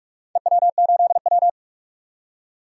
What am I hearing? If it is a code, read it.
Morse code, heard as a keyed tone, about 42 wpm, 700 Hz. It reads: EW8W
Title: DXCC-100 Call Signs - 4 Characters - 45wpm